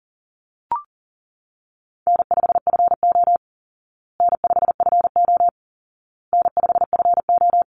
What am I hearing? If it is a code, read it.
Morse code, heard as a keyed tone, about 40 wpm, 700 Hz. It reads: N5FO N5FO N5FO
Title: Call Signs Repeated 2x - 40wpm